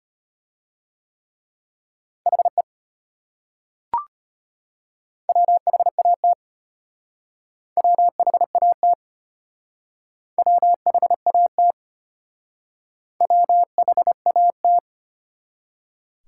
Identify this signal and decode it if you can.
Morse code, heard as a keyed tone, about 32 wpm, 700 Hz. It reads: HE WHAT WHAT WHAT WHAT